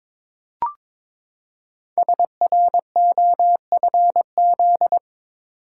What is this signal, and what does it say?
Morse code, heard as a keyed tone, about 22 wpm, 700 Hz. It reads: SROFZ
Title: Code Groups - 5 Letters - 22wpm